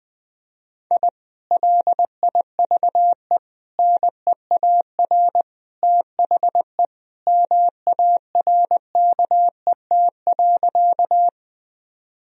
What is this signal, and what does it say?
Morse code, heard as a keyed tone, about 21 wpm, 700 Hz. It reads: I LIVE NEAR THE MARKET.
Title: Sentences from Top 1000 Words - Review Part 3 - 20wpm